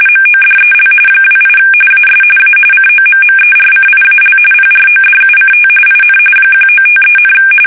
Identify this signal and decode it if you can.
RTTY, about 45.45 baud. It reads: MODEL 28 AS A COMMERCIAL PRODUCT IN 1953,
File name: fsk850 / teletype-corp